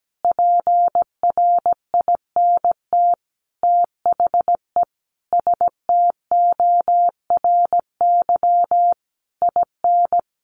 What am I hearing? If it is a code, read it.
Morse code, heard as a keyed tone, about 17 wpm, 700 Hz. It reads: PRINT THE STORY IN
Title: Sentences from Top 1400 Words 17wpm